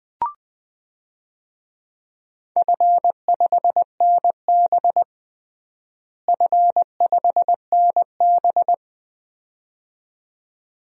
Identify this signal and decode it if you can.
Morse code, heard as a keyed tone, about 20 wpm, 700 Hz. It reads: F5NB F5NB
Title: DXCC-100 Call Signs - 4 Characters - Repeated 2x - 20wpm